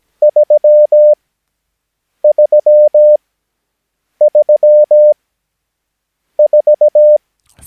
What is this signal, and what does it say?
Morse code, heard as a keyed tone, about 17 wpm, spaced as for 6 wpm, 600 Hz. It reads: 3334
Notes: Övningsmaterial med alfabetet A till Ö samt siffror 0 till 9 enligt International/Continental Morse Code.